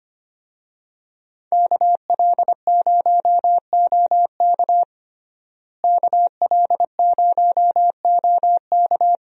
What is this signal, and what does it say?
Morse code, heard as a keyed tone, about 25 wpm, 700 Hz. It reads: KL0OK KL0OK
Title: USA Call Signs - 5 Characters - Repeated 2x - 25wpm